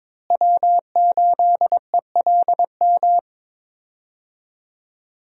Morse code, heard as a keyed tone, about 22 wpm, 700 Hz. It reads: W8ELM
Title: USA Call Signs - 5 Characters - Repeated 2x - 22wpm